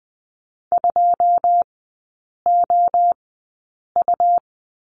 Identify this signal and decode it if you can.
Morse code, heard as a keyed tone, about 20 wpm, spaced as for 8 wpm, 690 Hz. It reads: 2OU
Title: 145 Lesson 24e - Letter y - Three Characters 20wpm